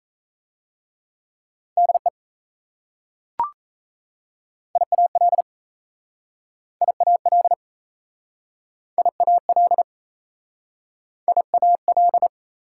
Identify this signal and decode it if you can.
Morse code, heard as a keyed tone, about 37 wpm, 700 Hz. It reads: DE IAL IAL IAL IAL